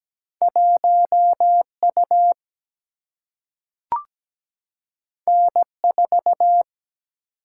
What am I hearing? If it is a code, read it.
Morse code, heard as a keyed tone, about 17 wpm, 700 Hz. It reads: 1 U N 4
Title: Single-Letter Single-Number 17wpm